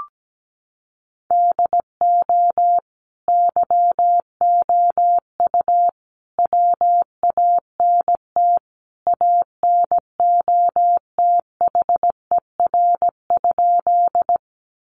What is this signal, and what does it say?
Morse code, heard as a keyed tone, about 17 wpm, 700 Hz. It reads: DO YOU WANT ANOTHER?
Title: Sentences from Top 600 Words 17wpm